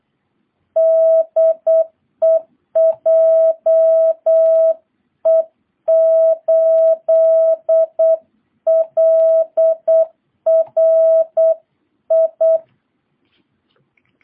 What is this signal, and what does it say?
Morse code, heard as a keyed tone, about 8 wpm, 660 Hz. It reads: DE JE8LRI